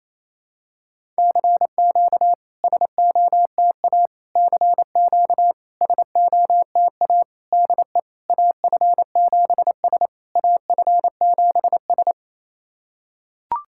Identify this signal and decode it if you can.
Morse code, heard as a keyed tone, about 29 wpm, 690 Hz. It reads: CQ SOTA CQ SOTA DE AF7S AF7S
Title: SOTA QSO Practice 28wpm